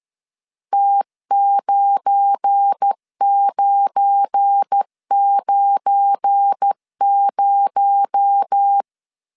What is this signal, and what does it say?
Morse code, heard as a keyed tone, about 13 wpm, 800 Hz. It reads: T9990
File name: LEZIONI / 7 GF9Z / 7 GF9 12wpm 2319